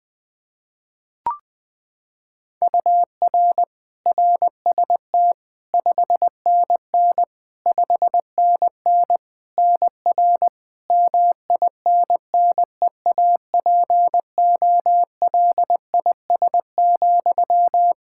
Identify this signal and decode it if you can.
Morse code, heard as a keyed tone, about 20 wpm, 700 Hz. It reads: UR RST 5NN 5NN NR MINNEAPOLIS,
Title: Practice QSO - Exchange 1 - Beginning - 20wpm